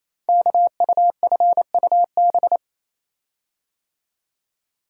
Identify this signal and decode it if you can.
Morse code, heard as a keyed tone, about 29 wpm, 700 Hz. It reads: KUFUB
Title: Code Groups - 5 Letters - 28wpm